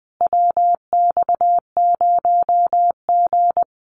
Morse code, heard as a keyed tone, about 20 wpm, 700 Hz. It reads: WX0G
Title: Common QSO Phrases Encore 20wpm